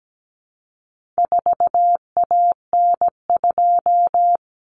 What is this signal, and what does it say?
Morse code, heard as a keyed tone, about 17 wpm, 700 Hz. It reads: 4AN2
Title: Common Cut Numbers - 4-Digit 17wpm